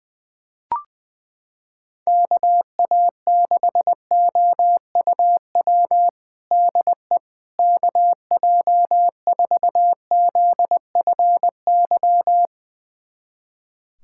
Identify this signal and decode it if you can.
Morse code, heard as a keyed tone, about 20 wpm, 700 Hz. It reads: KA6OUW DE KJ4ZFY
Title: USA Callsign DE Callsign - 6 Characters - 20wpm